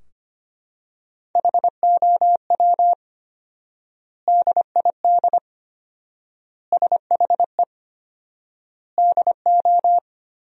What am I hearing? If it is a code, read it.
Morse code, heard as a keyed tone, about 25 wpm, 700 Hz. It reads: HOW DID SHE DO
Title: 4x Word-Spacing - Sentences from Top 100 Words - 25wpm